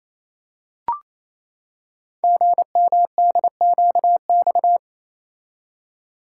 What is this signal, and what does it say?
Morse code, heard as a keyed tone, about 27 wpm, 700 Hz. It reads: GMDQX
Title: Code Groups - 5 Letters - 28wpm